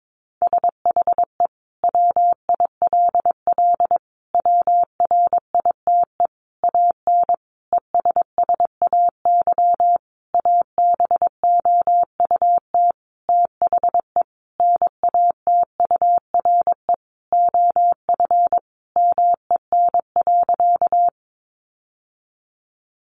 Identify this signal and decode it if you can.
Morse code, heard as a keyed tone, about 22 wpm, 700 Hz. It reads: SHE WILL WRITE AN ESSAY ABOUT THE NATURE OF MEN.
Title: Sentences from Top 1500 Words 22wpm